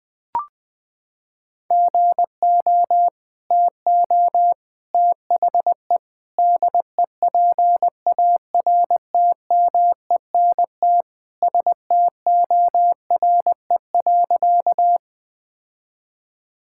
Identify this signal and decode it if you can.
Morse code, heard as a keyed tone, about 21 wpm, 700 Hz. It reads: GO TO THE DEPARTMENT STORE.
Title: Sentences from Top 1000 Words - Review Part 5 - 20wpm